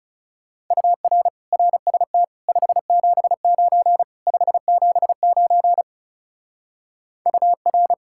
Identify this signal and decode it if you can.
Morse code, heard as a keyed tone, about 35 wpm, 700 Hz. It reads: UR RST 579 579 UR
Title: Speed-Racing Common QSO Phrases - 30wpm